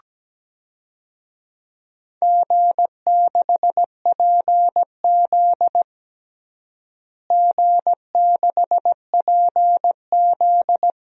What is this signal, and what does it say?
Morse code, heard as a keyed tone, about 17 wpm, 700 Hz. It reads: G6PZ G6PZ
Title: DXCC-100 Call Signs - 4 Characters - Repeated 3x - 17wpm